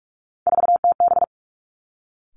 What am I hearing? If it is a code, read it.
Morse code, heard as a keyed tone, about 46 wpm, 700 Hz. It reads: 4TB
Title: Three Challenging Letter-Number Combinations 45wpm